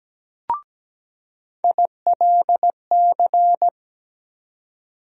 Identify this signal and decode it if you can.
Morse code, heard as a keyed tone, about 17 wpm, 700 Hz. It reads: ILC
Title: USA Call Sign Postfixes - 3 Letters 17wpm